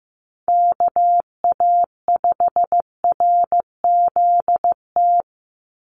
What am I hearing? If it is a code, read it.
Morse code, heard as a keyed tone, about 15 wpm, 700 Hz. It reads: KA5RZT
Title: USA Call Signs 15wpm